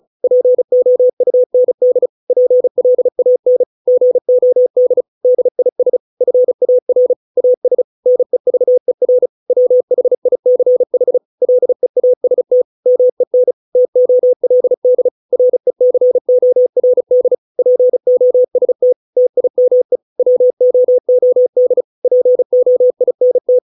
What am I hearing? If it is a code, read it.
Morse code, heard as a keyed tone, about 34 wpm, 500 Hz. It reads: POUND PLAN GOD DIS FAR AS NEVER WHICH LEAST MEN TOLD RECORD POST TIME WOOD POINT